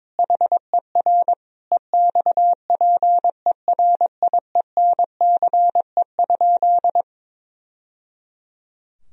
Morse code, heard as a keyed tone, about 22 wpm, 700 Hz. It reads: HER EXPERIENCE?
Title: Sentences from Top 1000 Words - Review Part 3 - 22wpm